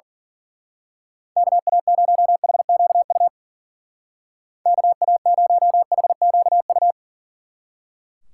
Morse code, heard as a keyed tone, about 47 wpm, 700 Hz. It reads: KA0HQU KA0HQU
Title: Speed Racing - USA 2x3 Call Signs 40wpm